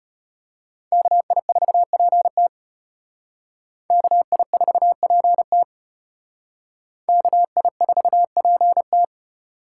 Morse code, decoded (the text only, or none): KI4PT KI4PT KI4PT